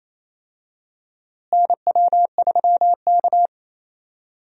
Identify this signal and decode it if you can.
Morse code, heard as a keyed tone, about 28 wpm, 700 Hz. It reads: NW3K
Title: USA Call Signs - 4 Characters - Repeated 2x - 28wpm